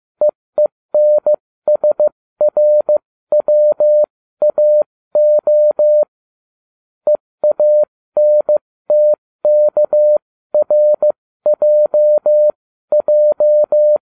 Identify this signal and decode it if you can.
Morse code, heard as a keyed tone, about 15 wpm, spaced as for 13 wpm, 600 Hz. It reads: EENSRWAO EANTKRJJ